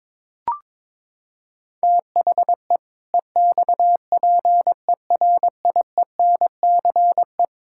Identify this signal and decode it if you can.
Morse code, heard as a keyed tone, about 22 wpm, 700 Hz. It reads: THE EXPERIENCE
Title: Sentences from Top 1000 Words - Review Part 5 - 22wpm